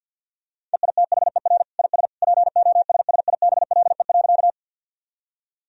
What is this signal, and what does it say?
Morse code, heard as a keyed tone, about 50 wpm, 700 Hz. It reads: EITHER IS POSSIBLE.
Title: Sentences from Top 1000 Words - Review Part 4 - 50wpm